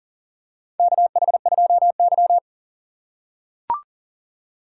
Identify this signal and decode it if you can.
Morse code, heard as a keyed tone, about 38 wpm, 700 Hz. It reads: KH2Y
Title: USA Call Signs - 4 Characters - 3x Character Spacing - 40wpm